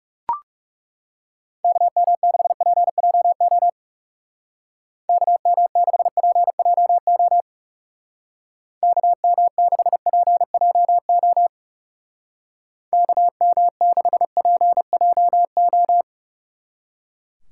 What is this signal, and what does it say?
Morse code, heard as a keyed tone, about 38 wpm, 700 Hz. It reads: KM6PJO KM6PJO KM6PJO KM6PJO